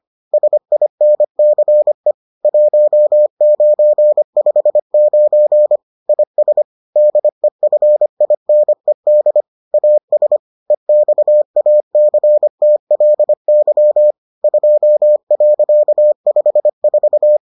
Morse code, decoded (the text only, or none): SINCE 1959 IS DEFINED AS EXACTLY 2.54